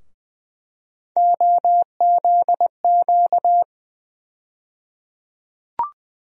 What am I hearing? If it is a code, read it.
Morse code, heard as a keyed tone, about 20 wpm, 700 Hz. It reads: OZQ
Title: USA Call Sign Postfixes - 3 Letters 20wpm